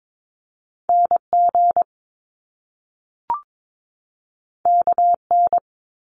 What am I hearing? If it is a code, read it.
Morse code, heard as a keyed tone, about 22 wpm, 700 Hz. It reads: NG KN
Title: Top Two Letter Combinations in Words 22wpm